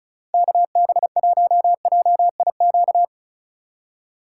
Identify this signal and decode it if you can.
Morse code, heard as a keyed tone, about 36 wpm, 700 Hz. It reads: KB1JIQ
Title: USA Call Signs - 4x Character Spacing - 35wpm